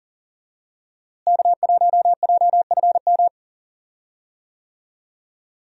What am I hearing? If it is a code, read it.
Morse code, heard as a keyed tone, about 40 wpm, 700 Hz. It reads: K1JFM
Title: USA Call Signs - 5 Characters - Repeated 2x - 40wpm